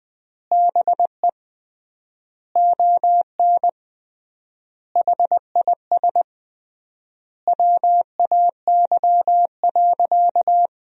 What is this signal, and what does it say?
Morse code, heard as a keyed tone, about 20 wpm, 700 Hz. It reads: BE ON HIS WAY.
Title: 3x Word-Spacing - Sentences from Top 300 Words - 20wpm